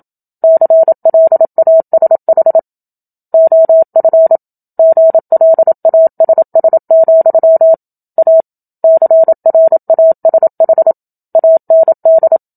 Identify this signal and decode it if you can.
Morse code, heard as a keyed tone, about 27 wpm, 650 Hz. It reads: CLASH OF GLASS, A CRASH AND
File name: Book Two / 01-Chapter 1 - 27 WPM